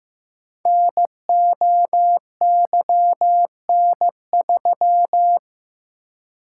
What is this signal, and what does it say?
Morse code, heard as a keyed tone, about 15 wpm, 700 Hz. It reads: NOYN3
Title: Code Groups - 5 Letter:Numbers - 15wpm